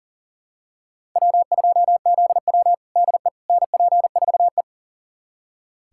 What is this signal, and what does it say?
Morse code, heard as a keyed tone, about 40 wpm, 690 Hz. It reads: W2ZW DE NP4E
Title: USA Callsign DE Callsign - 4 Characters - 40wpm